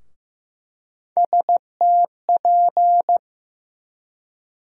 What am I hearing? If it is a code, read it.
Morse code, heard as a keyed tone, about 15 wpm, 700 Hz. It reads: STP
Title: USA Call Sign Postfixes - 3 Letters 15wpm